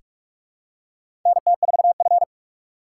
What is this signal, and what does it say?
Morse code, heard as a keyed tone, about 46 wpm, 700 Hz. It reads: NT4F
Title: USA Call Signs - 4 Characters - 4x Character Spacing - 45wpm